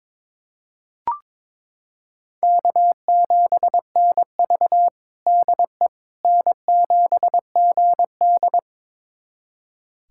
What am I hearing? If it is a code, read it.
Morse code, heard as a keyed tone, about 22 wpm, 700 Hz. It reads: K7NV DE N7GD
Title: USA Callsign DE Callsign - 4 Characters - 22wpm